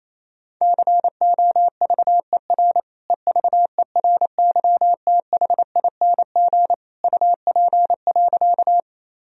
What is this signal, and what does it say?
Morse code, heard as a keyed tone, about 28 wpm, 700 Hz. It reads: COVER EVERYTHING UP.
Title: Sentences from Top 1000 Words - Review Part 1 - 28wpm